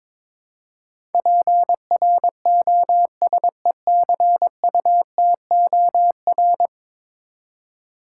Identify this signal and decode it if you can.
Morse code, heard as a keyed tone, about 22 wpm, 700 Hz. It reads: PROSECUTOR
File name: practice / 22wpm / 10 Letter Words - First 3 Letters Given - 22wpm